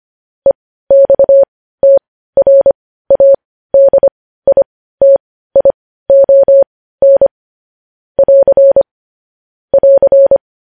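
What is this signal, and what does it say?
Morse code, heard as a keyed tone, about 25 wpm, spaced as for 15 wpm, 550 Hz. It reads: EXTRADITION <AR> <AR>
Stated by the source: Morse news headlines at 15 wpm.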